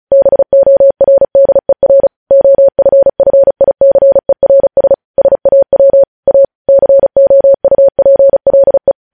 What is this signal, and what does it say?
Morse code, heard as a keyed tone, about 36 wpm, 550 Hz. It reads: BORDER OFFICERS SAW A COUPLE
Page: News headlines in Morse code at 35 wpm